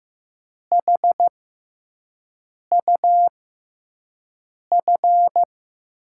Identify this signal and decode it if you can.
Morse code, heard as a keyed tone, about 15 wpm, spaced as for 5 wpm, 700 Hz. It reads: HUF
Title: USA Call Signs - 5 Characters - 6x Character Spacing - 15wpm